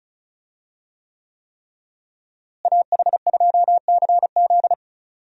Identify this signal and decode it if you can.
Morse code, heard as a keyed tone, about 34 wpm, 700 Hz. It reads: AH2CZ
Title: USA Call Signs - 5 Characters - Repeated 3x - 35wpm